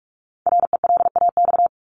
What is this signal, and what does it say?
Morse code, heard as a keyed tone, about 44 wpm, 700 Hz. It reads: RELAX
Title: Top 1251-1500 Words 45wpm